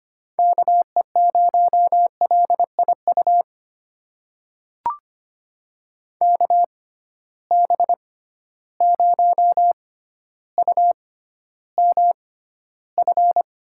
Morse code, heard as a keyed tone, about 25 wpm, spaced as for 13 wpm, 700 Hz. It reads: KE0LIU KB0UMF
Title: USA Call Signs - 6 Characters - 6x Character Spacing - 25wpm